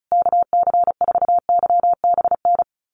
Morse code, heard as a keyed tone, about 34 wpm, 700 Hz. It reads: KC4YBN
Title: USA Call Signs Repeated 3x 35wpm